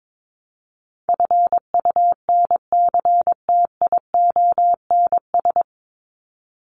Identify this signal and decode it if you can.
Morse code, heard as a keyed tone, about 22 wpm, 700 Hz. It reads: FUNCTIONS
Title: ES:S Suffix - Top 100 Words with Base Words 22wpm